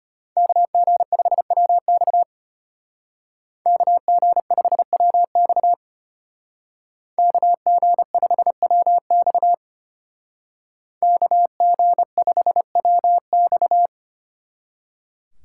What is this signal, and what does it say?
Morse code, heard as a keyed tone, about 33 wpm, 700 Hz. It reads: KG5WX KG5WX KG5WX KG5WX